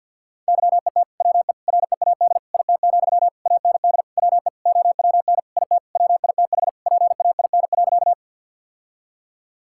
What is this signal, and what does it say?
Morse code, heard as a keyed tone, about 50 wpm, 700 Hz. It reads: YET WE READ IT, AND WE OWN IT WITH PAIN.
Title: Sentences from Top 1600 Words 50wpm